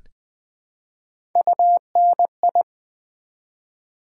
Morse code, heard as a keyed tone, about 20 wpm, 700 Hz. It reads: UNI
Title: Build A Word - 1 Letter At A Time - 5-Letter Words - 20wpm